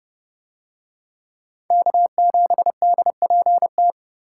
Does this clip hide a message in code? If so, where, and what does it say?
Yes: Morse code from 1.7 to 3.9 s, heard as a keyed tone, about 29 wpm, 700 Hz. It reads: K7DPT